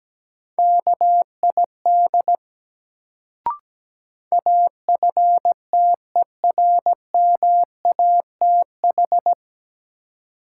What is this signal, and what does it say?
Morse code, heard as a keyed tone, about 17 wpm, 700 Hz. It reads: KID AFTERMATH
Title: Sets of 5 Words - New Word Drill - 17wpm